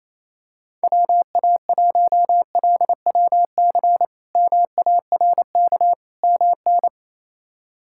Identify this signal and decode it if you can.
Morse code, heard as a keyed tone, about 28 wpm, 700 Hz. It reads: WA1LWC MARK MN
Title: K1USN SST QSO Practice - 28wpm